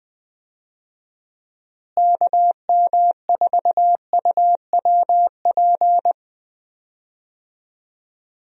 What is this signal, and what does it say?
Morse code, heard as a keyed tone, about 20 wpm, 700 Hz. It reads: KM4UWP
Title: USA Call Signs - 6 Characters - Repeated 3x - 20wpm